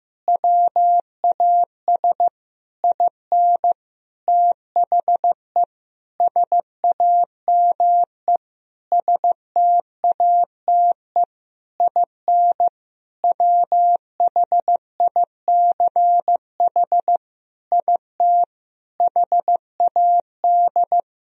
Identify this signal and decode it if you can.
Morse code, heard as a keyed tone, about 15 wpm, 700 Hz. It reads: WAS IN THE SAME STATE IN WHICH IT HAD
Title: Sentences from Top 1000 Words - Review Part 2 - 15wpm